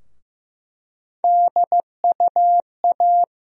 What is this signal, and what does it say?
Morse code, heard as a keyed tone, about 15 wpm, 700 Hz. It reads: DUA
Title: USA Call Sign Postfixes - 3 Letters 15wpm